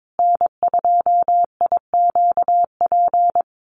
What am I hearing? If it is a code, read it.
Morse code, heard as a keyed tone, about 22 wpm, 700 Hz. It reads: N2IQP
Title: USA Call Signs Repeated 3x 22wpm